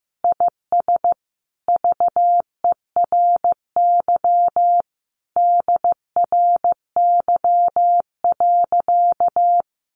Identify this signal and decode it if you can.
Morse code, heard as a keyed tone, about 15 wpm, 700 Hz. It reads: IS VERY DRY.
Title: Sentences from Top 1400 Words 15wpm